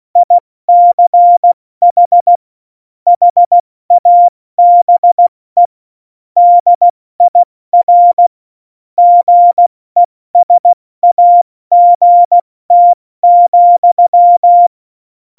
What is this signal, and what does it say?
Morse code, heard as a keyed tone, about 16 wpm, 700 Hz. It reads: ICH HABE DIR GESAGT,
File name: Morsekurs / Witze / Witz 29. Oktober 2019, 70 F80